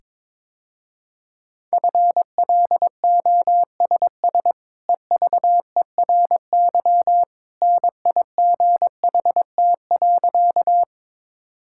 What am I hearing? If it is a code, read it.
Morse code, heard as a keyed tone, about 22 wpm, 700 Hz. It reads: FLOSS EVERY NIGHT.
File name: practice / 22wpm / Easy Sentences with Hints - 22wpm